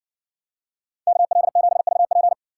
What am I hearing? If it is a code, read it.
Morse code, heard as a keyed tone, about 55 wpm, 700 Hz. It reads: KL7VP